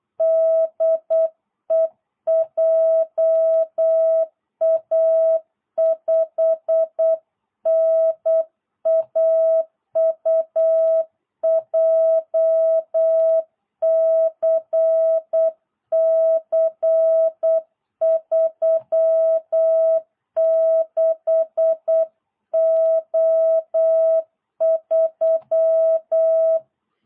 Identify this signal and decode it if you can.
Morse code, heard as a keyed tone, about 8 wpm, 660 Hz. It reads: DE JA5NAU JCC36O3
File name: jh8cbh / 2022cw